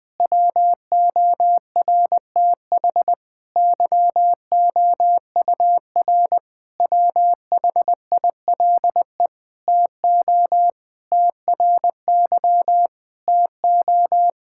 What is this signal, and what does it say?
Morse code, heard as a keyed tone, about 20 wpm, 700 Hz. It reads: WORTH YOUR WHILE TO TRY TO
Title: Sentences from Top 1600 Words 20wpm